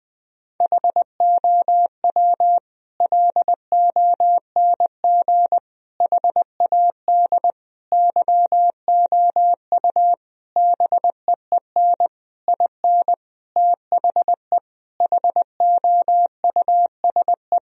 Morse code, heard as a keyed tone, about 20 wpm, 700 Hz. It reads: HOW LONG HAD YOU BEEN IN THE HOUSE
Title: Sentences from Top 1000 Words - Review Part 3 - 20wpm